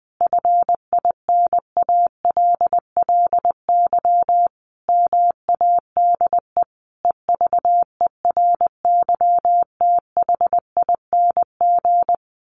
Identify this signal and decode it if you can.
Morse code, heard as a keyed tone, about 20 wpm, 700 Hz. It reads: FINALLY MADE EVERYTHING
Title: Sentences from Top 1300 Words 20wpm